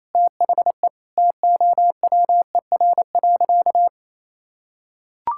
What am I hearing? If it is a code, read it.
Morse code, heard as a keyed tone, about 28 wpm, 700 Hz. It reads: THE TOWER.
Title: Pangram Challenge v2 - 28wpm